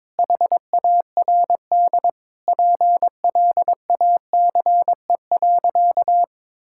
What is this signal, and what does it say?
Morse code, heard as a keyed tone, about 22 wpm, 700 Hz. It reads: HARD PLACE.
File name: practice / 22wpm / Sentences with 8-Word Phrases - 22wpm